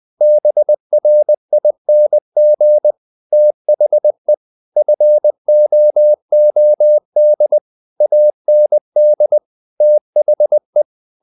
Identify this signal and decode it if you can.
Morse code, heard as a keyed tone, about 20 wpm, 600 Hz. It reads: BRING THE FOOD AND THE
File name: the old man and the sea part 2 20wpm